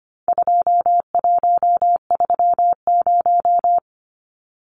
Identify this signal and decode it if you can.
Morse code, heard as a keyed tone, about 25 wpm, 700 Hz. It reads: 2130
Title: Numbers - 4-Digit 25wpm